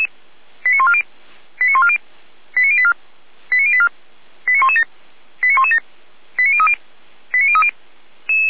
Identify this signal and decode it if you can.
Five-tone selective call (ZVEI), heard as quick runs of five tones, tones of about 70 ms. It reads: 89140 89140 89084 89084 89117 89117 89030 89030